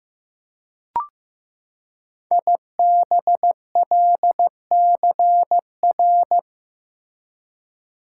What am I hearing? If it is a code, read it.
Morse code, heard as a keyed tone, about 15 wpm, 700 Hz. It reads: IBLCR
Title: Code Groups - 5 Letter:Numbers - 15wpm